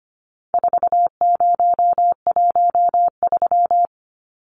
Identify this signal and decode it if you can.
Morse code, heard as a keyed tone, about 25 wpm, 700 Hz. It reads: 4013
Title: Numbers - 4-Digit 25wpm